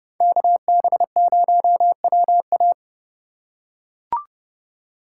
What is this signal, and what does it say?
Morse code, heard as a keyed tone, about 28 wpm, 700 Hz. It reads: KB0WA